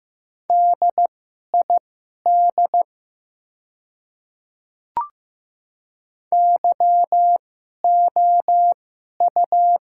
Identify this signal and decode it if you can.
Morse code, heard as a keyed tone, about 15 wpm, spaced as for 11 wpm, 700 Hz. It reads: DID YOU
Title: 2x Character Spacing - Top 100 Words - 15wpm